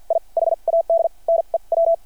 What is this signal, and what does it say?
Morse code, heard as a keyed tone, about 52 wpm, spaced as for 39 wpm, 660 Hz. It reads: I HAD NEW
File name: archived / CRSnet